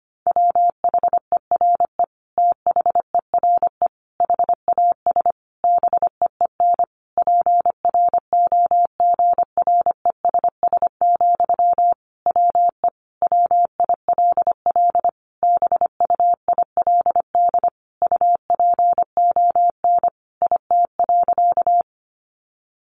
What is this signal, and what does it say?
Morse code, heard as a keyed tone, about 25 wpm, 700 Hz. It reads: WHERE THERE HAS BEEN PROGRESS, WE WILL BUILD UPON IT.
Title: Sentences from Top 1400 Words 25wpm